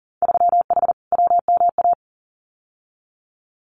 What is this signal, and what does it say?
Morse code, heard as a keyed tone, about 40 wpm, 700 Hz. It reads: 3H WMA
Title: Winter Field Day Exchanges Encore 40wpm